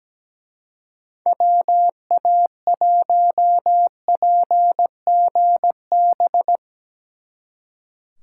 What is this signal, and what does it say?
Morse code, heard as a keyed tone, about 19 wpm, 700 Hz. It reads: WA1PGB
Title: Speed Racing - USA 2x3 Call Signs 17wpm